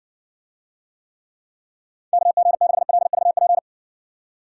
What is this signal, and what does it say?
Morse code, heard as a keyed tone, about 60 wpm, 700 Hz. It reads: KG6LVP